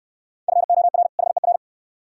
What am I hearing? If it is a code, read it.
Morse code, heard as a keyed tone, about 69 wpm, 700 Hz. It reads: FOR HER